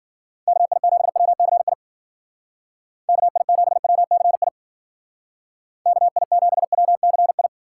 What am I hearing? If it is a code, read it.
Morse code, heard as a keyed tone, about 52 wpm, 700 Hz. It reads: KI7WCI KI7WCI KI7WCI